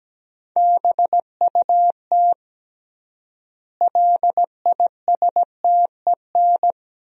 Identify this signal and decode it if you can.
Morse code, heard as a keyed tone, about 17 wpm, 700 Hz. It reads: BUT LISTEN
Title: 3x Word-Spacing - Sentences from Top 400 Words - 17wpm